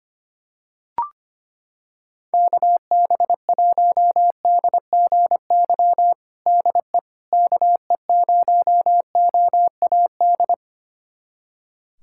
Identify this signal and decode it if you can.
Morse code, heard as a keyed tone, about 25 wpm, 700 Hz. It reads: KB1DGY DE KE0OAD
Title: USA Callsign DE Callsign - 6 Characters - 25wpm